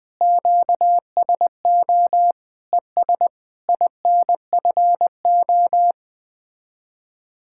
Morse code, heard as a keyed tone, about 20 wpm, 700 Hz. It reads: QSO ES INFO
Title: Speed-Racing Common QSO Phrases - 17wpm